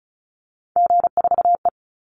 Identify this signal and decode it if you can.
Morse code, heard as a keyed tone, about 34 wpm, 700 Hz. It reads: G4E
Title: Call Signs Repeated 3x - 35wpm